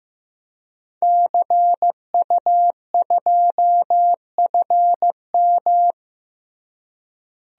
Morse code, heard as a keyed tone, about 15 wpm, 700 Hz. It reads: CU2FM
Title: DXCC-100 Call Signs - 5 Characters - Repeated 3x - 15wpm